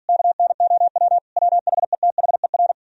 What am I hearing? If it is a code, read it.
Morse code, heard as a keyed tone, about 56 wpm, 700 Hz. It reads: KNOW WHETHER